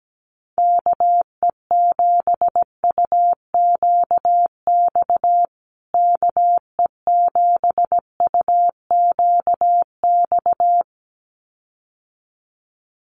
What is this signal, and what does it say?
Morse code, heard as a keyed tone, about 17 wpm, 700 Hz. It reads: KE7UQX KE7UQX
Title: USA Call Signs Repeated 2x 17wpm